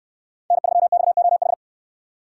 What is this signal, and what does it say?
Morse code, heard as a keyed tone, about 68 wpm, 700 Hz. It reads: N3XOH